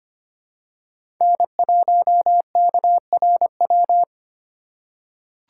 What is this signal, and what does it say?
Morse code, heard as a keyed tone, about 25 wpm, 700 Hz. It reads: N1KRW
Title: USA Call Signs - 3x Character Spacing - 25wpm